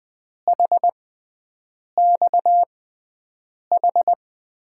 Morse code, heard as a keyed tone, about 20 wpm, spaced as for 7 wpm, 700 Hz. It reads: HXH
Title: USA Call Signs - 6 Characters - 6x Character Spacing - 20wpm